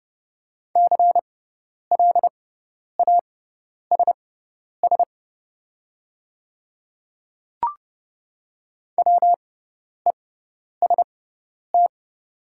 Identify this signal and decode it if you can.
Morse code, heard as a keyed tone, about 30 wpm, spaced as for 10 wpm, 700 Hz. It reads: CLASS WEST
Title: 6x Character Spacing - Top 500 Words - 30wpm